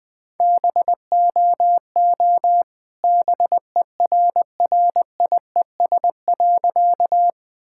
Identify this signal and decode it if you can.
Morse code, heard as a keyed tone, about 20 wpm, 700 Hz. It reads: BOO BERRIES.
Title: All-Age Appropriate Jokes - 20wpm